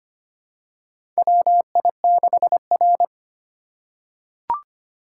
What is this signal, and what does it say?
Morse code, heard as a keyed tone, about 25 wpm, 700 Hz. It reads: WI6R
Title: USA Call Signs - 4 Characters - 6x Character Spacing - 25wpm